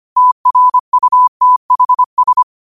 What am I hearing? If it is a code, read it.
Morse code, heard as a keyed tone, about 25 wpm, 1000 Hz. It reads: TRUTHS
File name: qotdfiles / 25wpm